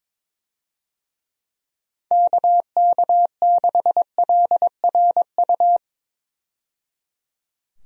Morse code, heard as a keyed tone, about 22 wpm, 700 Hz. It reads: KK6LRU
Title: USA Call Signs - 6 Characters - 22wpm